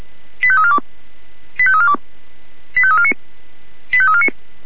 Five-tone selective call (ZVEI), heard as quick runs of five tones, tones of about 70 ms. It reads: 85342 85342 85349 85349